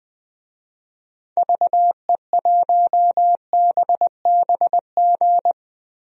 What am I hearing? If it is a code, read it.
Morse code, heard as a keyed tone, about 20 wpm, 700 Hz. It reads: VE1BBG
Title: DXCC-100 Call Signs - 6 Characters - Repeated 3x - 20wpm